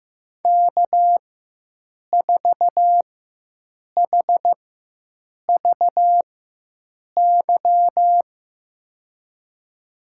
Morse code, heard as a keyed tone, about 15 wpm, spaced as for 7 wpm, 700 Hz. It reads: K4HVY
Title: USA Call Signs - 5 Characters - 4x Character Spacing - 15wpm